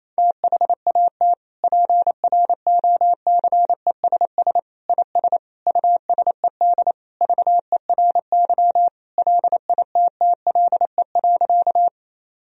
Morse code, heard as a keyed tone, about 28 wpm, 700 Hz. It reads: THAT PROCESS IS USED VERY LITTLE.
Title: Sentences from Top 1000 Words - Review Part 3 - 28wpm